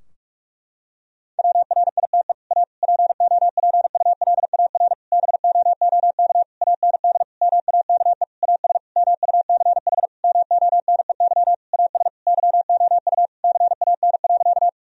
Morse code, heard as a keyed tone, about 46 wpm, 700 Hz. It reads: WRITE A POPULAR BOOK AND MAKE AS MUCH MONEY AS YOU CAN.
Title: Sentences from Top 1000 Words - Review Part 3 - 45wpm